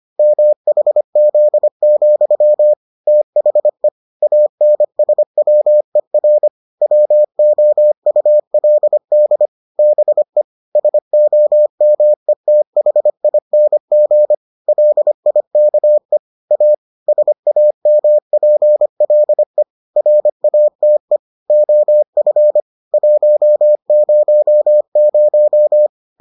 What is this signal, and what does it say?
Morse code, heard as a keyed tone, about 25 wpm, 600 Hz. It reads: MHZ, THE ANSWER WOULD BE SOMETHING LIKE A SAMPLE RATE OF 100